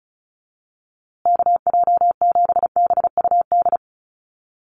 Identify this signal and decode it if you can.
Morse code, heard as a keyed tone, about 34 wpm, 700 Hz. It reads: KJ7BUD
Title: USA Call Signs Repeated 2x 35wpm